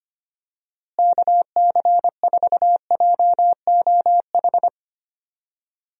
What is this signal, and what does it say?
Morse code, heard as a keyed tone, about 25 wpm, 700 Hz. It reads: KC4JOH
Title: USA Call Signs - 6 Characters - Repeated 2x - 25wpm